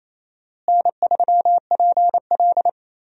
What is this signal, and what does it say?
Morse code, heard as a keyed tone, about 28 wpm, 700 Hz. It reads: N3PL
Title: USA Call Signs - 4 Characters - Repeated 3x - 28wpm